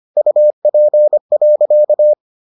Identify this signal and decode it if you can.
Morse code, heard as a keyed tone, about 25 wpm, 600 Hz. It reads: UP.